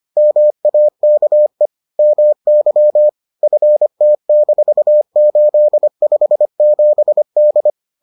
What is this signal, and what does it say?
Morse code, heard as a keyed tone, about 25 wpm, 600 Hz. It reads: MAKE MY FT-857D